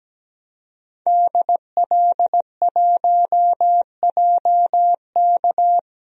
Morse code, heard as a keyed tone, about 17 wpm, 700 Hz. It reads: DL1JK
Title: DXCC-100 Call Signs - 5 Characters - 17wpm